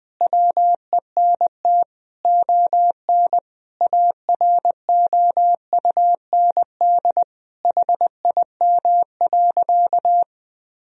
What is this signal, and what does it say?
Morse code, heard as a keyed tone, about 20 wpm, 700 Hz. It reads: WENT ON AROUND HIM.
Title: Sentences from Top 1000 Words - Review Part 3 - 20wpm